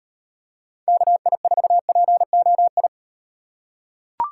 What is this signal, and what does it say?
Morse code, heard as a keyed tone, about 38 wpm, 700 Hz. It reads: KI4POI